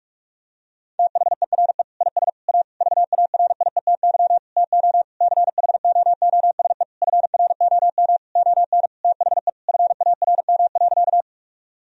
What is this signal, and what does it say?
Morse code, heard as a keyed tone, about 46 wpm, 700 Hz. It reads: THERE IS A VARIETY TO CHOOSE FROM ON THE FARM.
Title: Sentences from Top 1000 Words - Review Part 4 - 45wpm